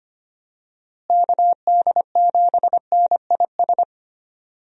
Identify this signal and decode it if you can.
Morse code, heard as a keyed tone, about 25 wpm, 700 Hz. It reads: KD7NIS
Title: USA Call Signs - 6 Characters - Repeated 2x - 25wpm